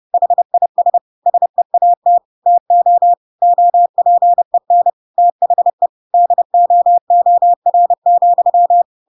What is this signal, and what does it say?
Morse code, heard as a keyed tone, about 30 wpm, 700 Hz. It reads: HIS SEAT TO OPEN THE DOOR,
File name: Pinocchio / Pinocchio ch7 30 30 ik1hge